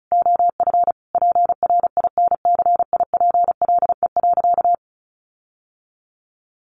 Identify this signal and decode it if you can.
Morse code, heard as a keyed tone, about 34 wpm, 700 Hz. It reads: OF PRINCIPLE.
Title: Sentences from Top 1400 Words 35wpm